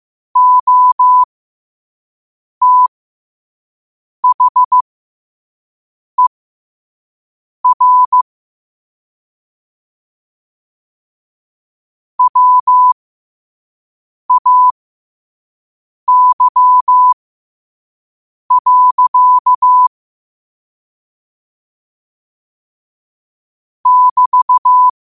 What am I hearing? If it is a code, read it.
Morse code, heard as a keyed tone, about 15 wpm, spaced as for 5 wpm, 1000 Hz. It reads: OTHER WAY. <BT>
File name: qotdfiles / 5wpm